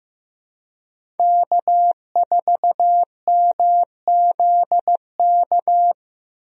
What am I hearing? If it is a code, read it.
Morse code, heard as a keyed tone, about 16 wpm, 700 Hz. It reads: K4MZK
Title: USA Call Signs - 5 Characters - 15wpm